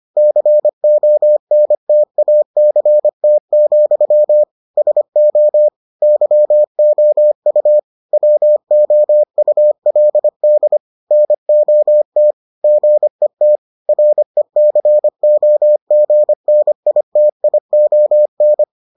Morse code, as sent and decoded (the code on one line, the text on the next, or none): -.-. --- -. - .- -.-. - --..-- / ... --- / -.-- --- ..- / .-- --- ..- .-.. -.. / -. --- - / --. - / .-. -.-. --- --. -. .. - .. --- -.
CONTACT, SO YOU WOULD NOT GET RECOGNITION